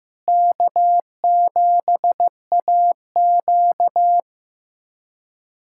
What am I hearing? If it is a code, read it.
Morse code, heard as a keyed tone, about 15 wpm, 700 Hz. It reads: K7AQ
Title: USA Callsign DE Callsign - 4 Characters - 15wpm